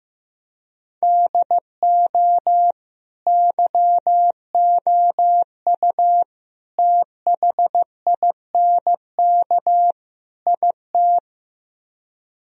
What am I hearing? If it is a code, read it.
Morse code, heard as a keyed tone, about 15 wpm, 700 Hz. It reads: DO YOU THINK IT
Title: Speed-Racing - Sets of 4 Words 15wpm